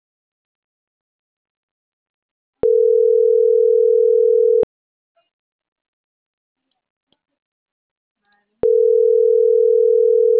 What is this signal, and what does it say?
Ringback tone (North America), 440 + 480 Hz.